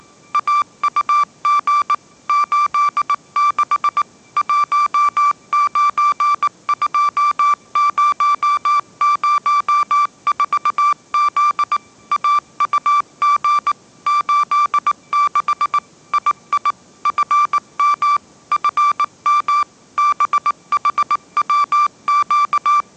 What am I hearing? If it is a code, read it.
Morse code, heard as a keyed tone, about 19 wpm, 1170 Hz. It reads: AUG 86 192004Z AUG 86 II FM FM BHWQ
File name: 20wpm